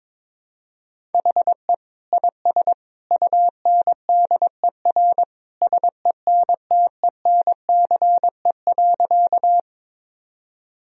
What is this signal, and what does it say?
Morse code, heard as a keyed tone, about 22 wpm, 700 Hz. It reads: HE IS UNDER SENTENCE.
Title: Sentences from Top 1000 Words - Review Part 1 - 22wpm